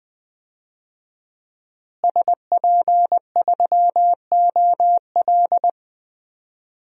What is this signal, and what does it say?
Morse code, heard as a keyed tone, about 21 wpm, 700 Hz. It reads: SP3OL
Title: DXCC-100 Call Signs - 5 Characters - Repeated 2x - 20wpm